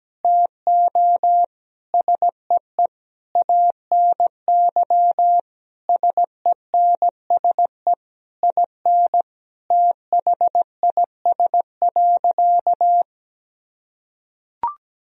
Morse code, heard as a keyed tone, about 17 wpm, 700 Hz. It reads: TO SEE ANY SENSE IN THIS.
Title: Sentences from Top 1000 Words - Review Part 4 - 17wpm